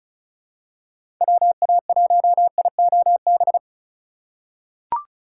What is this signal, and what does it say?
Morse code, heard as a keyed tone, about 36 wpm, 700 Hz. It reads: WA1IOB
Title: USA Call Signs - 4x Character Spacing - 35wpm